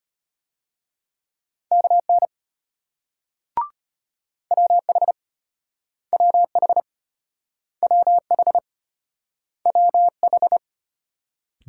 Morse code, heard as a keyed tone, about 35 wpm, 700 Hz. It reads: KN WH WH WH WH